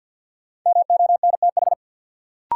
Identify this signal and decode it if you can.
Morse code, heard as a keyed tone, about 52 wpm, 700 Hz. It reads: MONTH
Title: Sentences from Top 800 Words - New Word Drill - 50wpm